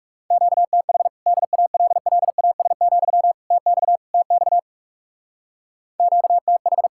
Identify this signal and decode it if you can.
Morse code, heard as a keyed tone, about 36 wpm, 700 Hz. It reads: QTH DALLAS, TX TX QTH
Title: Speed-Racing Common QSO Phrases Encore - 30wpm